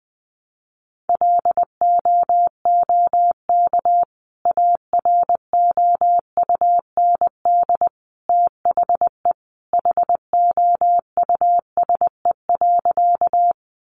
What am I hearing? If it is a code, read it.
Morse code, heard as a keyed tone, about 20 wpm, 700 Hz. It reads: LOOK AROUND THE HOUSE.
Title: Sentences from Top 600 Words 20wpm